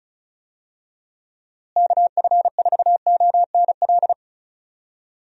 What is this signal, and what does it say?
Morse code, heard as a keyed tone, about 36 wpm, 700 Hz. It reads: KF4ONL
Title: USA Call Signs - 6 Characters - Repeated 2x - 35wpm